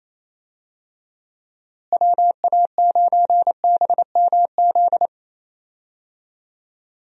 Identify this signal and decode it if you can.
Morse code, heard as a keyed tone, about 27 wpm, 700 Hz. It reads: WA9BMZ
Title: USA Call Signs - 6 Characters - Repeated 2x - 28wpm